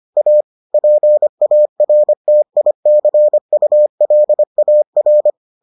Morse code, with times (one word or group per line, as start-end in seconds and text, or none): 0.2-0.4 A
0.7-5.3 PARTICULAR